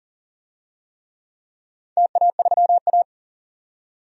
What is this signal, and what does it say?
Morse code, heard as a keyed tone, about 40 wpm, 700 Hz. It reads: TA3A
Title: DXCC-100 Call Signs - 4 Characters - 40wpm